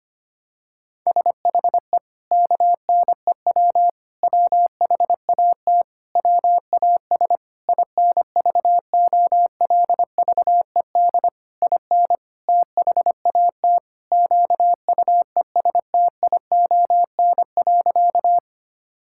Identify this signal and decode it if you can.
Morse code, heard as a keyed tone, about 25 wpm, 700 Hz. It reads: SHE KNEW WHAT WAS INVOLVED IN THAT QUESTION.
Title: Sentences from Top 1000 Words - Review Part 5 - 25wpm